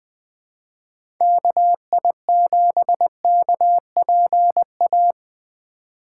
Morse code, heard as a keyed tone, about 20 wpm, 700 Hz. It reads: KI7KPA
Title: USA Call Signs - 6 Characters - 3x Character Spacing - 20wpm